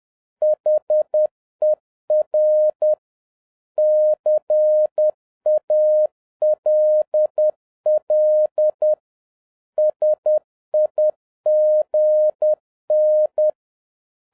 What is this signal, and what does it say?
Morse code, heard as a keyed tone, about 10 wpm, 610 Hz. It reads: HER CALL SIGN